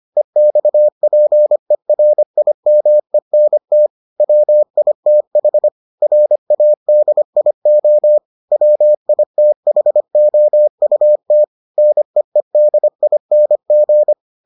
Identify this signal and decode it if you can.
Morse code, heard as a keyed tone, about 25 wpm, 600 Hz. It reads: EXPERIMENT WITH RADIO WITHOUT NEEDING